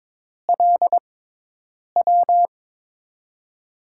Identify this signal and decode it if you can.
Morse code, heard as a keyed tone, about 23 wpm, spaced as for 8 wpm, 700 Hz. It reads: LW
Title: USA Call Signs - 5 Characters - 6x Character Spacing - 22wpm